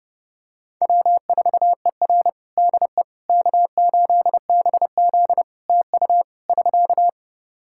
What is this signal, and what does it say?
Morse code, heard as a keyed tone, about 30 wpm, 700 Hz. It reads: W4ER DE K8BZ TU <SK>
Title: Practice QSO - Exchange 4 - Ending - 30wpm